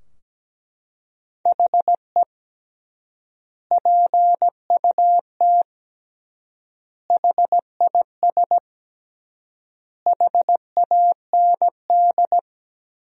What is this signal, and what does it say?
Morse code, heard as a keyed tone, about 17 wpm, 700 Hz. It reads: HE PUT HIS HAND
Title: 3x Word-Spacing - Sentences from Top 400 Words - 17wpm